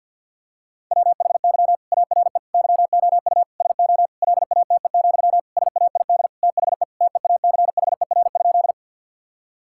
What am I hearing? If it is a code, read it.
Morse code, heard as a keyed tone, about 49 wpm, 700 Hz. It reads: WHY ARE YOU SO LATE, SAID THE TEACHER?
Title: Sentences from Top 1000 Words - Review Part 6 - 50wpm